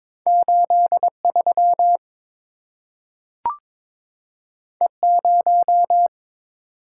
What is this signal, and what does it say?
Morse code, heard as a keyed tone, about 22 wpm, 700 Hz. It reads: 83 E0
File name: practice / 22wpm / Two Letter-Number Combinations - 22wpm